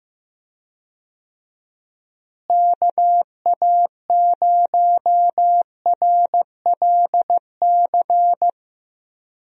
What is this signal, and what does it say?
Morse code, heard as a keyed tone, about 15 wpm, 700 Hz. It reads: KA0RLC
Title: USA Call Signs - 6 Characters - Repeated 3x - 15wpm